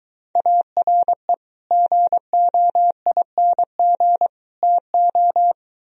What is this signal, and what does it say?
Morse code, heard as a keyed tone, about 23 wpm, 700 Hz. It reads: ARE GOING TO